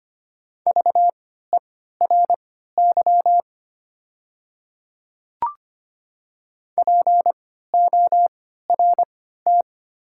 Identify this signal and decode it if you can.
Morse code, heard as a keyed tone, about 25 wpm, spaced as for 14 wpm, 690 Hz. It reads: VERY PORT
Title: 3x Character Spacing - Top 300 Words - 25wpm